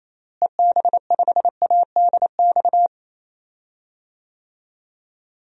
Morse code, heard as a keyed tone, about 28 wpm, 700 Hz. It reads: EB5ADX
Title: DXCC-100 Call Signs - 6 Characters - Repeated 2x - 28wpm